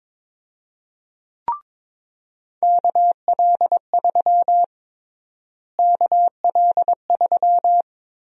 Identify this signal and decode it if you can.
Morse code, heard as a keyed tone, about 22 wpm, 700 Hz. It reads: KL3 KL3
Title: USA Call Sign Prefixes - Repeated 2x - 22wpm